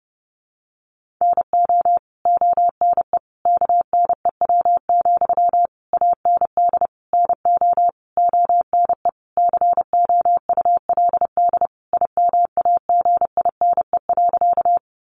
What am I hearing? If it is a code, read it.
Morse code, heard as a keyed tone, about 30 wpm, 700 Hz. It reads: NO ONE KNEW, AND NO ONE COULD IMAGINE.
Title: Sentences from Top 1400 Words 30wpm